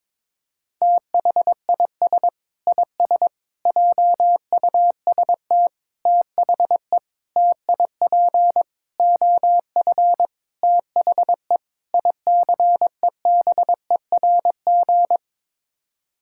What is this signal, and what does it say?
Morse code, heard as a keyed tone, about 22 wpm, 700 Hz. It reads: THIS IS JUST THE TIP OF THE ICEBERG
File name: practice / 22wpm / Sets of 8 Words - 22wpm